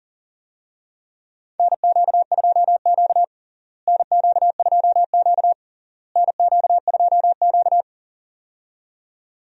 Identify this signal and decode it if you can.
Morse code, heard as a keyed tone, about 40 wpm, 700 Hz. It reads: NQ2Q NQ2Q NQ2Q
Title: USA Call Signs - 4 Characters - Repeated 3x - 40wpm